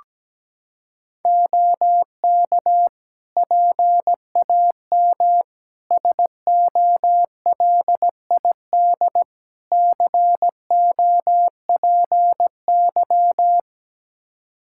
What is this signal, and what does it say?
Morse code, heard as a keyed tone, about 17 wpm, 700 Hz. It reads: OK PAM SOLID COPY
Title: K1USN SST QSO Practice 17wpm